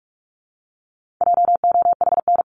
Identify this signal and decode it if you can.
Morse code, heard as a keyed tone, about 44 wpm, 700 Hz. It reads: JOHN
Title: CWT Exchange 45wpm